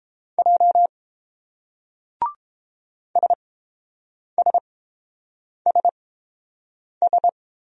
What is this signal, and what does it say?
Morse code, heard as a keyed tone, about 30 wpm, spaced as for 7 wpm, 700 Hz. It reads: J SSSS